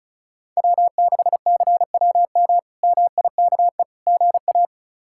Morse code, heard as a keyed tone, about 34 wpm, 700 Hz. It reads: W6CWM MIKE GA
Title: K1USN SST QSO Practice - 35wpm